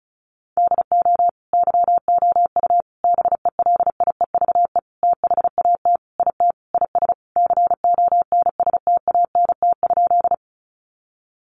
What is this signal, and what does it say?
Morse code, heard as a keyed tone, about 34 wpm, 700 Hz. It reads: DO YOU BELIEVE THAT IT IS CONSTANT?
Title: Sentences from Top 1500 Words 35wpm